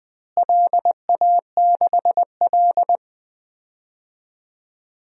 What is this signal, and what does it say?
Morse code, heard as a keyed tone, about 20 wpm, 700 Hz. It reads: LA6L
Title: DXCC-100 Call Signs - 4 Characters - Repeated 3x - 20wpm